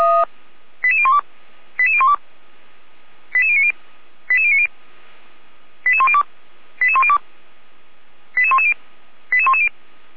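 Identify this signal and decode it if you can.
Five-tone selective call (ZVEI), heard as quick runs of five tones, tones of about 70 ms. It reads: 88012 88012 88090 88090 88102 88102 88110 88110
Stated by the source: ZVEI five-tone call 88247 received by SDR, from the server of Feuerwehr Orlamünde (fire brigade).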